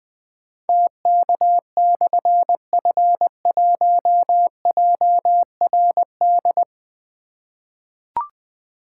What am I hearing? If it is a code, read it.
Morse code, heard as a keyed tone, about 20 wpm, 700 Hz. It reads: TK/F1JRD
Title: Difficult Call Signs 20wpm